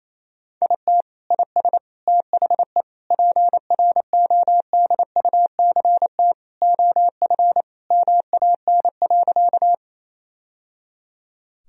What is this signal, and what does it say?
Morse code, heard as a keyed tone, about 28 wpm, 700 Hz. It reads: IT IS THE PRODUCT OF MAN.
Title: Sentences from Top 1000 Words - Review Part 2 - 28wpm